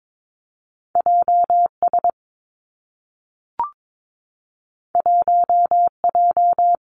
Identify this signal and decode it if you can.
Morse code, heard as a keyed tone, about 22 wpm, 700 Hz. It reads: J S 1 J
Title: Two Challenging Letter-Number Combinations 22wpm